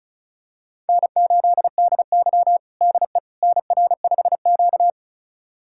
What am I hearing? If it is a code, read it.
Morse code, heard as a keyed tone, about 36 wpm, 700 Hz. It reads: N8DY DE NR5Q
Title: USA Callsign DE Callsign - 4 Characters - 35wpm